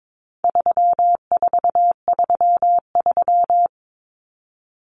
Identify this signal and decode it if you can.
Morse code, heard as a keyed tone, about 22 wpm, 700 Hz. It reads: 3433
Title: Numbers - 4-Digit 22wpm